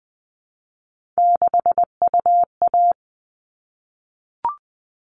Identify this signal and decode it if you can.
Morse code, heard as a keyed tone, about 20 wpm, 700 Hz. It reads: 6UA
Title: All Cut Numbers - 3-Digit 20wpm